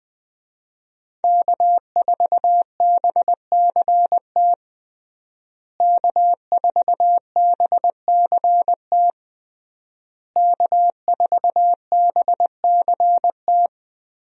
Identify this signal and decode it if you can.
Morse code, heard as a keyed tone, about 21 wpm, 700 Hz. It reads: K4BCT K4BCT K4BCT
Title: USA Call Signs - 5 Characters - Repeated 3x - 20wpm